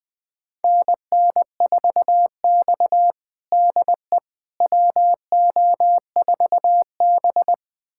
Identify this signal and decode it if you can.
Morse code, heard as a keyed tone, about 21 wpm, 700 Hz. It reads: NN4X DE WO4B
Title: USA Callsign DE Callsign - 4 Characters - 20wpm